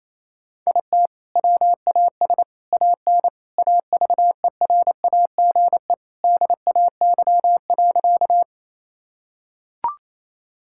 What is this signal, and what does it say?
Morse code, heard as a keyed tone, about 28 wpm, 700 Hz. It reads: IT WAS AN AVERAGE DAY.
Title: Sentences from Top 1000 Words - Review Part 6 - 28wpm